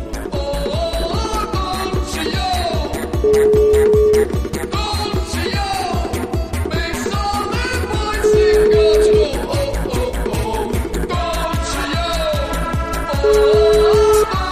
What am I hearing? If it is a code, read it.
Ringback tone (Europe, 425 Hz), with noise.